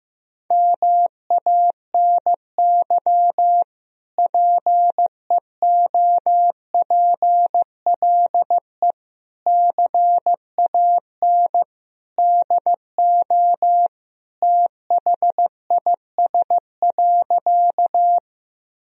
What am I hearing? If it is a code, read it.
Morse code, heard as a keyed tone, about 15 wpm, 700 Hz. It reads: MANY PEOPLE CAN DO THIS.
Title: Sentences from Top 1000 Words - Review Part 1 - 15wpm